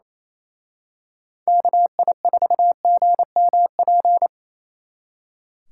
Morse code, heard as a keyed tone, about 28 wpm, 700 Hz. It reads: KI4GMP
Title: Speed Racing - USA Call Signs 28wpm